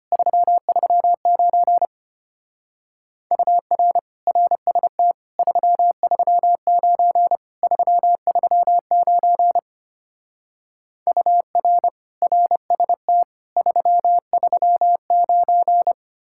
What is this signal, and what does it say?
Morse code, heard as a keyed tone, about 32 wpm, 700 Hz. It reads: 339 UR RST 339 339 UR RST 339